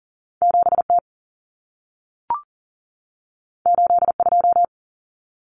Morse code, heard as a keyed tone, about 40 wpm, 700 Hz. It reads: 7T 82
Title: Common Cut Numbers - 2-Digit 40wpm